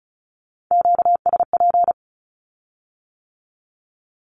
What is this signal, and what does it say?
Morse code, heard as a keyed tone, about 36 wpm, 700 Hz. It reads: QSP
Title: Q-Codes with Spoken Meaning - 35wpm